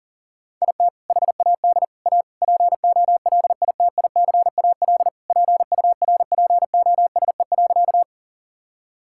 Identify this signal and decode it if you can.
Morse code, heard as a keyed tone, about 40 wpm, 700 Hz. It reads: IT HAD A POLITICAL PURPOSE.
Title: Sentences from Top 1000 Words - Review Part 5 - 40wpm